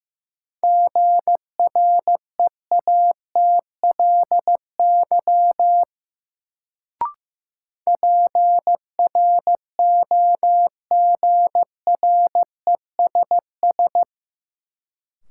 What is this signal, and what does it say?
Morse code, heard as a keyed tone, about 15 wpm, 700 Hz. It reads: GREATLY PROGRESS
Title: Sentences from Top 1400 Words - New Word Drill Alt - 15wpm